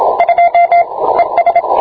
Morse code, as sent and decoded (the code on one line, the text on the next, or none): ..--- / . ...
2 ES